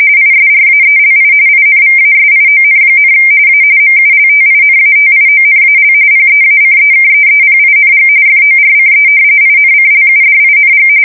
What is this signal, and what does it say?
RTTY, about 45.45 baud. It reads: ISRAELI INVASION, A MULTINATIONAL PEACEKEEPING FORCE INCLUDING
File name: fsk170 / battleshipnj